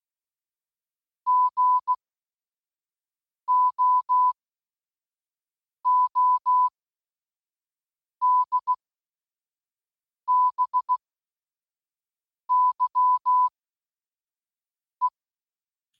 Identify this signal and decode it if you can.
Morse code, heard as a keyed tone, about 16 wpm, spaced as for 5 wpm, 1000 Hz. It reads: GOODBYE